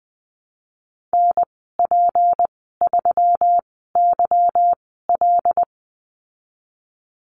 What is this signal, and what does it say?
Morse code, heard as a keyed tone, about 20 wpm, spaced as for 15 wpm, 700 Hz. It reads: NP3YL
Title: 161 Lesson 26g - Letter g - Call Signs 20wpm